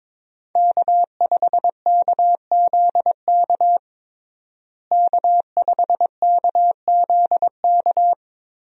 Morse code, heard as a keyed tone, about 23 wpm, 700 Hz. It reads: K5KZK K5KZK
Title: USA Call Signs - 5 Characters - Repeated 3x - 22wpm